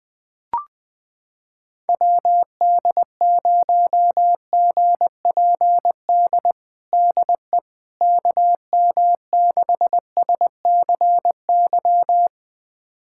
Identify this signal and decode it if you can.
Morse code, heard as a keyed tone, about 20 wpm, 700 Hz. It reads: WD0GPD DE KM6SCY
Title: USA Callsign DE Callsign 20wpm